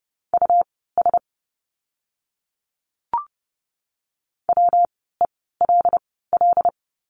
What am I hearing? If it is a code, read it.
Morse code, heard as a keyed tone, about 30 wpm, spaced as for 17 wpm, 700 Hz. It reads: US WELL
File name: course / 30:10wpm / 115 Lesson 19f - Letter w - Words 30wpm